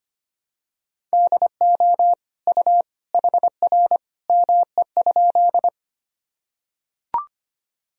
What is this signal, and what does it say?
Morse code, heard as a keyed tone, about 25 wpm, 700 Hz. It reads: DO U HR ME?
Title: Sentences from Top 300 Words and Abbreviations 25wpm